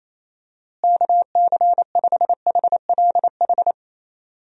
Morse code, heard as a keyed tone, about 28 wpm, 700 Hz. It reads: KC5HLH
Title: USA Call Signs - 6 Characters - Repeated 2x - 28wpm